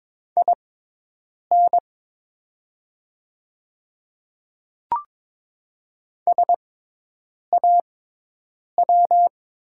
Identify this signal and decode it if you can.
Morse code, heard as a keyed tone, about 22 wpm, spaced as for 8 wpm, 700 Hz. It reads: IN SAW
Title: 6x Character Spacing - Top 300 Words - 22wpm